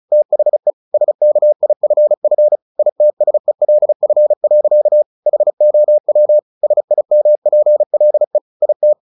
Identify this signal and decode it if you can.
Morse code, heard as a keyed tone, about 34 wpm, 600 Hz. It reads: THE SKIFF ITSELF. HOW SIMPLE IT
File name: the old man and the sea 2 35wpm